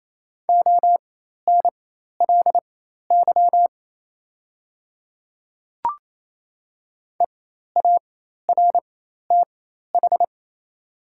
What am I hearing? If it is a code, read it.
Morse code, heard as a keyed tone, about 28 wpm, spaced as for 13 wpm, 700 Hz. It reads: ONLY EARTH
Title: 4x Character Spacing - Top 200 Words - 28wpm